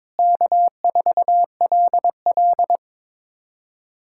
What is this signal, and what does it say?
Morse code, heard as a keyed tone, about 22 wpm, 700 Hz. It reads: K4LL
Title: USA Callsign DE Callsign - 4 Characters - 22wpm